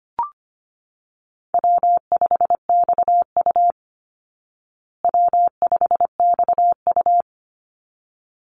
Morse code, heard as a keyed tone, about 25 wpm, 700 Hz. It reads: W5XU W5XU
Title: Easy Call Signs Repeated 2x - 25wpm